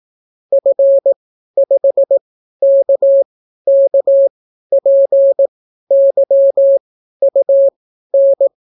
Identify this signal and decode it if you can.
Morse code, heard as a keyed tone, about 18 wpm, spaced as for 12 wpm, 550 Hz. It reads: F5KKPYUN